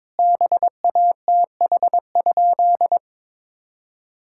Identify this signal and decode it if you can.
Morse code, heard as a keyed tone, about 22 wpm, 700 Hz. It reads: BATH?
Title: Sentences from Top 1600 Words 22wpm